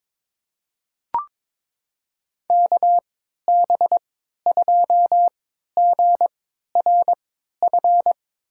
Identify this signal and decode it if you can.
Morse code, heard as a keyed tone, about 22 wpm, spaced as for 13 wpm, 700 Hz. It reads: KB2GRF
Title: USA Call Signs - 3x Character Spacing - 22wpm